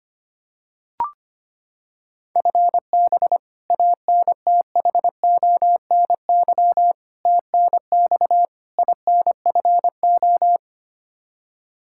Morse code, heard as a keyed tone, about 25 wpm, 700 Hz. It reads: FB ANTHONY TNX INFO
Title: K1USN SST QSO Practice 25wpm